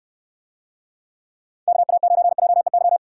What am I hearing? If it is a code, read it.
Morse code, heard as a keyed tone, about 68 wpm, 700 Hz. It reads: KA9JEY